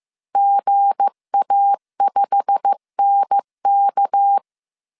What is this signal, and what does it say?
Morse code, heard as a keyed tone, about 15 wpm, 800 Hz. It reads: GA5NK
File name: CW / DET / 1 9 1500C S A 14wpm 3222